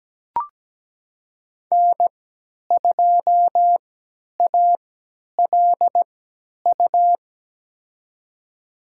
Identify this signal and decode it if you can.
Morse code, heard as a keyed tone, about 17 wpm, spaced as for 10 wpm, 700 Hz. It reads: N2ALU
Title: USA Call Signs - 3x Character Spacing - 17wpm